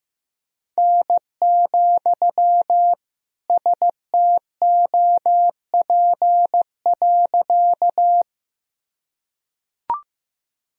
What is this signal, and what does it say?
Morse code, heard as a keyed tone, about 15 wpm, 700 Hz. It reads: N, STOP.
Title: Sentences from Top 300 Words and Abbreviations 15wpm